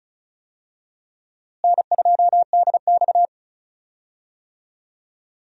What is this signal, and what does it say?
Morse code, heard as a keyed tone, about 36 wpm, 700 Hz. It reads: N2DX
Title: USA Call Signs - 4 Characters - Repeated 3x - 35wpm